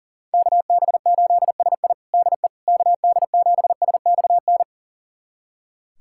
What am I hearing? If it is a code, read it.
Morse code, heard as a keyed tone, about 40 wpm, 700 Hz. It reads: KB8SI DE KD7SXN
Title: USA Callsign DE Callsign 40wpm